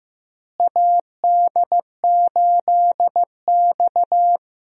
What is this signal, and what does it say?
Morse code, heard as a keyed tone, about 15 wpm, 700 Hz. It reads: AD8X
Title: USA Call Signs - 4 Characters - Repeated 3x - 15wpm